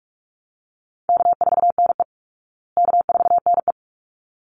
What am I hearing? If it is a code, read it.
Morse code, heard as a keyed tone, about 44 wpm, 700 Hz. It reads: K4NE K4NE
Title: Call Signs Repeated 3x - 45wpm